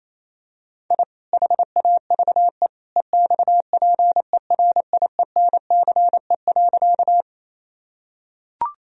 Morse code, heard as a keyed tone, about 27 wpm, 700 Hz. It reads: I HAVE EXPERIENCE.
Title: Sentences from Top 1000 Words - Review Part 3 - 28wpm